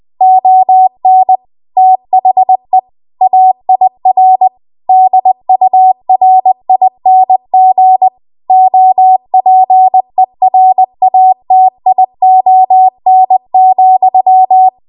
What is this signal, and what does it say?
Morse code, heard as a keyed tone, about 20 wpm, 750 Hz. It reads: ON THE AIR DURING OPERATION,
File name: Archive / 20 WPM / 20WPM